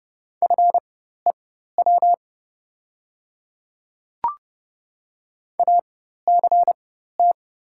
Morse code, heard as a keyed tone, about 30 wpm, spaced as for 14 wpm, 700 Hz. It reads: FEW ACT
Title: 4x Character Spacing - Top 300 Words - 30wpm